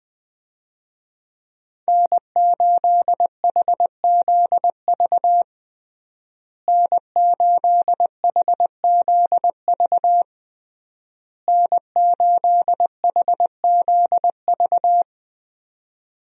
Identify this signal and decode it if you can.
Morse code, heard as a keyed tone, about 20 wpm, 700 Hz. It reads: N8HZV N8HZV N8HZV
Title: USA Call Signs - 5 Characters - Repeated 3x - 20wpm